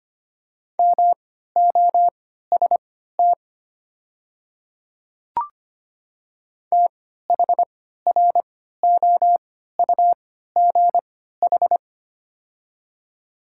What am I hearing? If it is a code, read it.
Morse code, heard as a keyed tone, about 25 wpm, spaced as for 14 wpm, 700 Hz. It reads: MOST THROUGH
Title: 3x Character Spacing - Top 200 Words - 25wpm